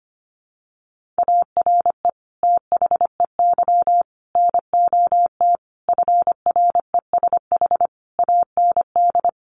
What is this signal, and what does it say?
Morse code, heard as a keyed tone, about 25 wpm, 700 Hz. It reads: ARE THEY NOT FRESH AND
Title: Sentences from Top 1200 Words 25wpm